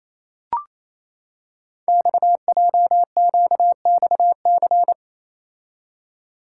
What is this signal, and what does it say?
Morse code, heard as a keyed tone, about 27 wpm, 700 Hz. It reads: XJQXC
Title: Code Groups - 5 Letters - 28wpm